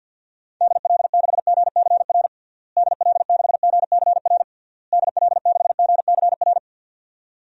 Morse code, heard as a keyed tone, about 51 wpm, 700 Hz. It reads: DL6GCR DL6GCR DL6GCR
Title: DXCC-100 Call Signs - 6 Characters - Repeated 3x - 50wpm